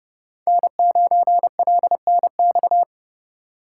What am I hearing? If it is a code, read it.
Morse code, heard as a keyed tone, about 34 wpm, 700 Hz. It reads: N9LNX
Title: Speed Racing - USA Call Signs 30wpm